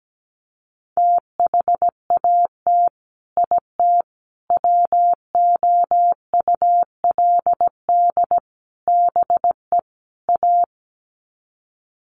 Morse code, heard as a keyed tone, about 17 wpm, 700 Hz. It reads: THAT IT WOULD BE A
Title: Sets of 5 Words Encore2 17wpm